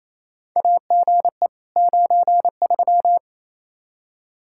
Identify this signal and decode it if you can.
Morse code, heard as a keyed tone, about 28 wpm, 700 Hz. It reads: AGE 93
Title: K1USN SST QSO Practice 28wpm